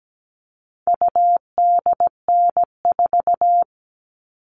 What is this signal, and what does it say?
Morse code, heard as a keyed tone, about 17 wpm, 700 Hz. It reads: UDN4
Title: All Cut Numbers - 4-Digit 17wpm